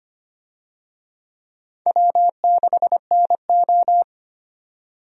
Morse code, heard as a keyed tone, about 25 wpm, 700 Hz. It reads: W6NO
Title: USA Call Signs - 4 Characters - 25wpm